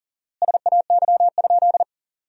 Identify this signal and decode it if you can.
Morse code, heard as a keyed tone, about 37 wpm, 700 Hz. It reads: SAY?
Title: 4x Word-Spacing - Sentences from Top 200 Words - 40wpm